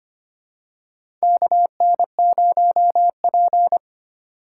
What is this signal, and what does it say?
Morse code, heard as a keyed tone, about 25 wpm, 700 Hz. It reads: KN0P
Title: USA Call Signs - 4 Characters - 3x Character Spacing - 25wpm